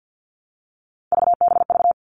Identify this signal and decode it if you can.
Morse code, heard as a keyed tone, about 48 wpm, 700 Hz. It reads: VBV
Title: All Cut Numbers - 3-Digit 50wpm